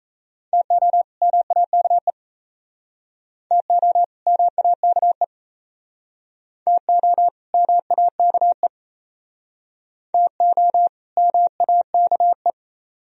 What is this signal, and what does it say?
Morse code, heard as a keyed tone, about 35 wpm, 700 Hz. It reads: TO MAKE TO MAKE TO MAKE TO MAKE